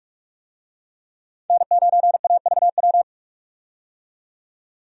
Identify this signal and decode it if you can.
Morse code, heard as a keyed tone, about 46 wpm, 700 Hz. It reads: N9AVW
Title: USA Call Signs - 5 Characters - 45wpm